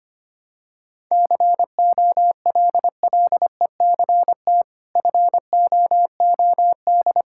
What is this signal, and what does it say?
Morse code, heard as a keyed tone, about 25 wpm, 700 Hz. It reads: COLLECT FOOD
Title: Sentences from Top 1600 Words 25wpm